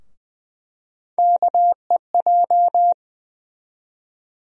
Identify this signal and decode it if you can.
Morse code, heard as a keyed tone, about 20 wpm, 700 Hz. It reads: KEJ
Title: USA Call Sign Postfixes - 3 Letters 20wpm